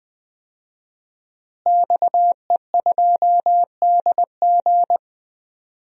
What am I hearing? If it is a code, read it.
Morse code, heard as a keyed tone, about 20 wpm, 700 Hz. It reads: XE2DG
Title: DXCC-100 Call Signs - 5 Characters - Repeated 2x - 20wpm